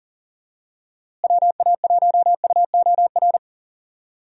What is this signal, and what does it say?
Morse code, heard as a keyed tone, about 40 wpm, 700 Hz. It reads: WA1UOR
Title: USA Call Signs - 6 Characters - Repeated 3x - 40wpm